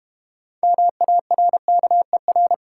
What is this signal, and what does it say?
Morse code, heard as a keyed tone, about 31 wpm, 700 Hz. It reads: MARKER
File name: Easy Words - Speed-Racing with Hints - 25wpm